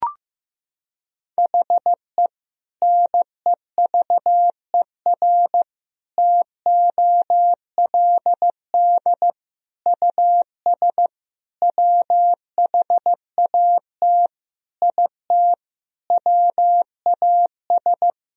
Morse code, heard as a keyed tone, about 15 wpm, 700 Hz. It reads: HE NEVER TOLD US WHAT IT WAS
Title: Sentences from Top 400 Words and Abbreviations 15wpm